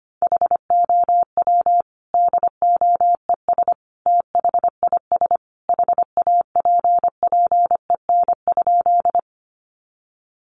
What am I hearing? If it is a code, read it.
Morse code, heard as a keyed tone, about 25 wpm, 700 Hz. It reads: HOW DOES THIS HAPPEN?
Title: Sentences from Top 400 Words 25wpm